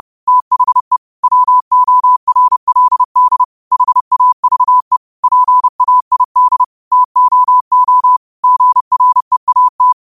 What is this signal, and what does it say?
Morse code, heard as a keyed tone, about 30 wpm, 1000 Hz. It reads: THE WORLD HAVE PAID TOO GREAT
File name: qotdfiles / 30wpm